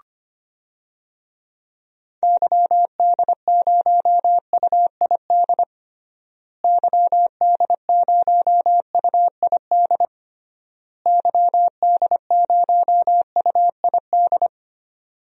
Morse code, heard as a keyed tone, about 25 wpm, 700 Hz. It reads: YD0UID YD0UID YD0UID
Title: DXCC-100 Call Signs - 6 Characters - Repeated 3x - 25wpm